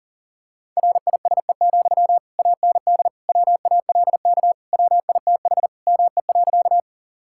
Morse code, heard as a keyed tone, about 40 wpm, 700 Hz. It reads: RISE, AND WALK WITH ME.
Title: Sentences from Top 1000 Words - Review Part 6 - 40wpm